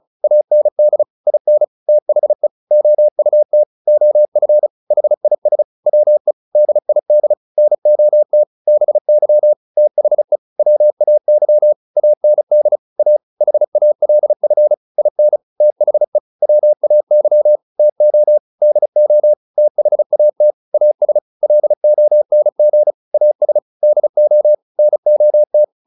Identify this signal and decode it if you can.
Morse code, heard as a keyed tone, about 34 wpm, 600 Hz. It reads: AND IN THE OUT OF HIS WE DID NOT BY THE WAY AND A HALF IN THE WAY TO DO THAT AS LONG AS DO NOT